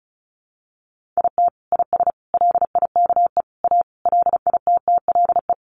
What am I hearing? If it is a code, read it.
Morse code, heard as a keyed tone, about 34 wpm, 700 Hz. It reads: IT IS LIKE A LITTLE
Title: Sentences from Top 1500 Words 35wpm